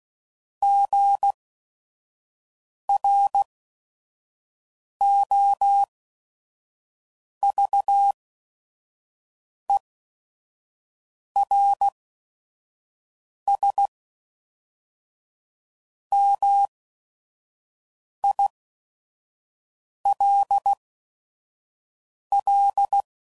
Morse code, heard as a keyed tone, about 16 wpm, spaced as for 5 wpm, 800 Hz. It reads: GROVERSMILL
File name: CW Code Groups @5WPM / fromSM's / Cities